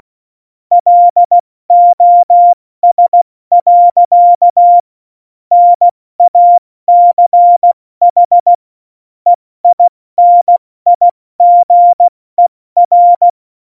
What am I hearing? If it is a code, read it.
Morse code, heard as a keyed tone, about 16 wpm, 700 Hz. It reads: LOS. NACH EINIGER
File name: Morsekurs / Witze / Witz 26 Mai, 2020 70 F80